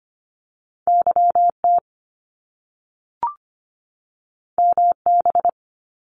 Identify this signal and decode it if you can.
Morse code, heard as a keyed tone, about 25 wpm, 700 Hz. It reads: YT MB
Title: Canadian Provinces and Territories 25wpm